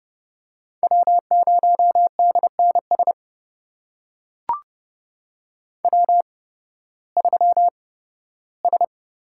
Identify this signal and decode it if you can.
Morse code, heard as a keyed tone, about 32 wpm, spaced as for 16 wpm, 700 Hz. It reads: W0DNS W3S
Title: USA Call Signs - 8x Character Spacing - 30wpm